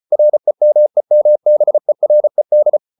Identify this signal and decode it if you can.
Morse code, heard as a keyed tone, about 34 wpm, 600 Hz. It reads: REMEMBERED